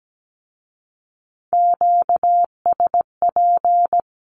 Morse code, heard as a keyed tone, about 17 wpm, 700 Hz. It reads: QSP
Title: Q-Codes with Spoken Meaning - 17wpm